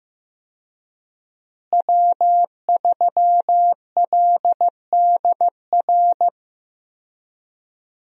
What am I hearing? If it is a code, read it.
Morse code, heard as a keyed tone, about 15 wpm, 700 Hz. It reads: W3LDR
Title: USA Call Signs - 5 Characters - Repeated 3x - 15wpm